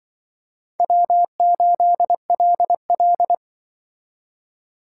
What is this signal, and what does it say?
Morse code, heard as a keyed tone, about 25 wpm, 700 Hz. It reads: W8LL
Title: Speed Racing - USA 1x2 Call Signs 20wpm